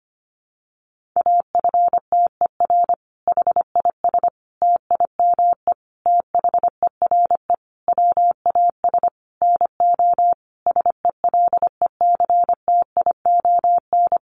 Morse code, heard as a keyed tone, about 25 wpm, 700 Hz. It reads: AFTER HIS TIME THERE WAS NO SELECTION
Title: Sentences from Top 1400 Words 25wpm